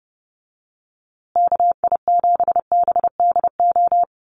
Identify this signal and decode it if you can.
Morse code, heard as a keyed tone, about 30 wpm, 700 Hz. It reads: KI7BDO
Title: USA Call Signs Repeated 2x 30wpm